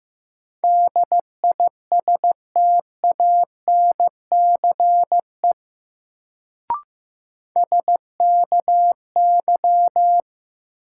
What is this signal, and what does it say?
Morse code, heard as a keyed tone, about 16 wpm, 700 Hz. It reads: DISTANCE SKY
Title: Sentences from Top 1200 Words - New Word Drill - 15wpm